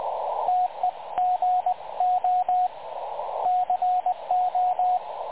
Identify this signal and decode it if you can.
Morse code, heard as a keyed tone, about 20 wpm, 730 Hz, with radio noise. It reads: TEGO CO
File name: dlazaawansowanych / nagraniazpasma / websdr recording 3565.0kHz